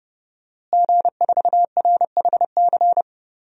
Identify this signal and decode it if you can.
Morse code, heard as a keyed tone, about 30 wpm, 700 Hz. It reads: G4RHC
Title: DXCC-100 Call Signs - 5 Characters - Repeated 3x - 30wpm